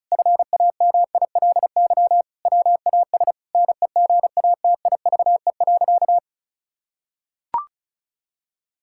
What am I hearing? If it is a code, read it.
Morse code, heard as a keyed tone, about 34 wpm, 700 Hz. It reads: FAMILY WAS NEGATIVE.
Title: Sentences from Top 1000 Words - Review Part 6 - 35wpm